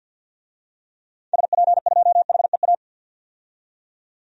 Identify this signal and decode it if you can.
Morse code, heard as a keyed tone, about 52 wpm, 700 Hz. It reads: SP2HEA
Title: DXCC-100 Call Signs - 6 Characters - 50wpm